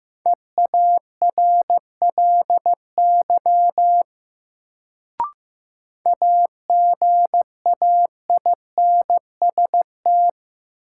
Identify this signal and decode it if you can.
Morse code, heard as a keyed tone, about 15 wpm, 700 Hz. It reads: EARLY AGAINST
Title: Sentences from Top 400 Words - New Word Drill - 15wpm